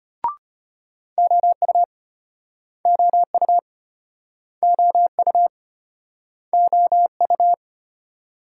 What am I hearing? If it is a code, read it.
Morse code, heard as a keyed tone, about 32 wpm, 700 Hz. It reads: OU OU OU OU